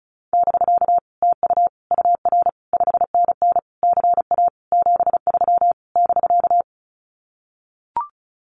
Morse code, heard as a keyed tone, about 34 wpm, 700 Hz. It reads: <BK> TU UR 5NN CA 73 <BK>
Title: Common QSO Phrases Encore 35wpm